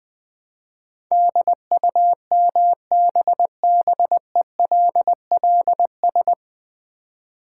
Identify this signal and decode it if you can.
Morse code, heard as a keyed tone, about 21 wpm, 700 Hz. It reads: DUMBBELLS
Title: Easy Words with Hints - 20wpm